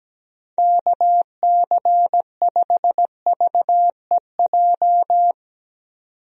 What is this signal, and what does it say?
Morse code, heard as a keyed tone, about 17 wpm, 700 Hz. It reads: KC5VEJ
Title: USA Call Signs - 6 Characters - 17wpm